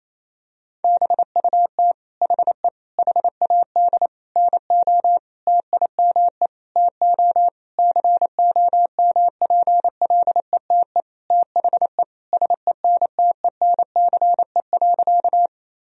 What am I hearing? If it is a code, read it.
Morse code, heard as a keyed tone, about 28 wpm, 700 Hz. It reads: BUT HE HAD NO TIME TO COMPLETE THE SENTENCE.
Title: Sentences from Top 1000 Words - Review Part 1 - 28wpm